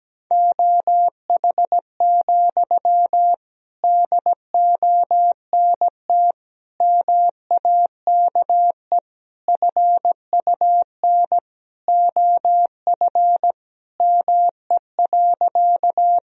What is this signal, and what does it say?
Morse code, heard as a keyed tone, about 17 wpm, 700 Hz. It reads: OH, DONT MAKE FUN OF ME.
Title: Sentences from Top 1000 Words - Review Part 4 - 17wpm